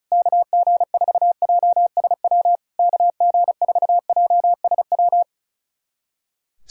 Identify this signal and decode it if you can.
Morse code, heard as a keyed tone, about 37 wpm, 700 Hz. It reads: KG4JSW KG4JSW
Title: SOTA QSO Practice 35wpm